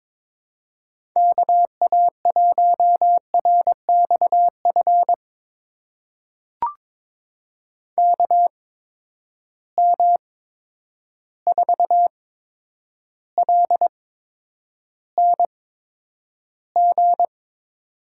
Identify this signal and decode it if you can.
Morse code, heard as a keyed tone, about 22 wpm, spaced as for 9 wpm, 700 Hz. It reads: KA1RXF KM4LNG
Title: USA Call Signs - 6 Characters - 8x Character Spacing - 22wpm